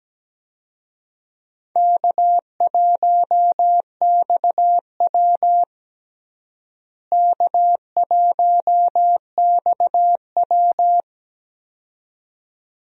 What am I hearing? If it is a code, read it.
Morse code, heard as a keyed tone, about 17 wpm, 700 Hz. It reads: K1XW K1XW
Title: USA Call Signs - 4 Characters - Repeated 2x - 17wpm